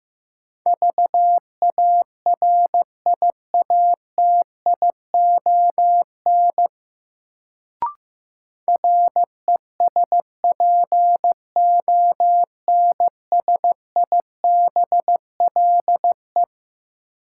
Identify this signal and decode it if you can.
Morse code, heard as a keyed tone, about 15 wpm, 700 Hz. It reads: VARIATION RESPONSIBLE
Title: Sentences from Top 1600 Words - New Word Drill Alt - 15wpm